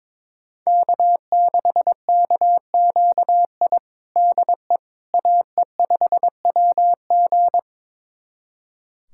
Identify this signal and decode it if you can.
Morse code, heard as a keyed tone, about 22 wpm, 700 Hz. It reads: K6KQI DE AE5WG
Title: USA Callsign DE Callsign - 5 Characters - 22wpm